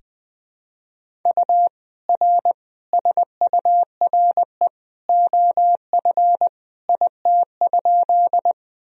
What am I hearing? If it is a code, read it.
Morse code, heard as a keyed tone, about 21 wpm, 700 Hz. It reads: U R SURE OF IT?
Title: Sentences from Top 300 Words and Abbreviations 20wpm